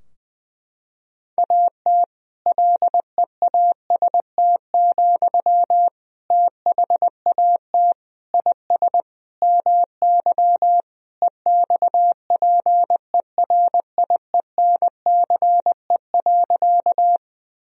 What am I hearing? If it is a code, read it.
Morse code, heard as a keyed tone, about 20 wpm, 700 Hz. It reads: AT LEAST, THAT IS MY EXPERIENCE.
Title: Sentences from Top 1000 Words - Review Part 3 - 20wpm